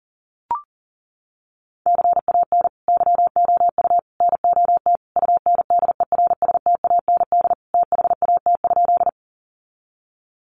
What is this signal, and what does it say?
Morse code, heard as a keyed tone, about 40 wpm, 700 Hz. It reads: CAN YOU NOT UNDERSTAND THAT?
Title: Sentences from Top 600 Words 40wpm